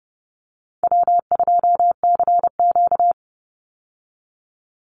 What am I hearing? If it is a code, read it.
Morse code, heard as a keyed tone, about 30 wpm, 700 Hz. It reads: W2CQ
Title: Easy Call Signs Repeated 2x - 30wpm